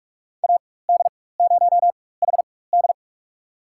Morse code, heard as a keyed tone, about 46 wpm, spaced as for 21 wpm, 700 Hz. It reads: AD0HD
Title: USA Call Signs - 5 Characters - 4x Character Spacing - 45wpm